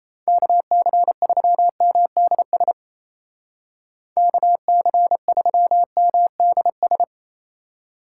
Morse code, heard as a keyed tone, about 35 wpm, 700 Hz. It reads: KC3MDS KC3MDS
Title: Speed Racing - USA 2x3 Call Signs 28wpm